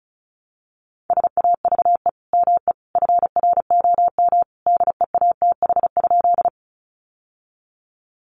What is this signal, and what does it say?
Morse code, heard as a keyed tone, about 34 wpm, 700 Hz. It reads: SAVE ME FROM DEATH?
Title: Sentences from Top 1100 Words 35wpm